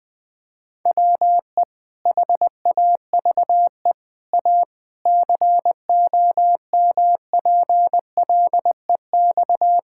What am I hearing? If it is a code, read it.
Morse code, heard as a keyed tone, about 21 wpm, 700 Hz. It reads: WE HAVE A COMPLEX
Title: Sentences from Top 1000 Words - Review Part 6 - 20wpm